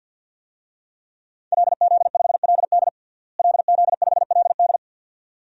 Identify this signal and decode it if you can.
Morse code, heard as a keyed tone, about 50 wpm, 700 Hz. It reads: LZ5LD LZ5LD
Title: DXCC-100 Call Signs - 5 Characters - Repeated 2x - 50wpm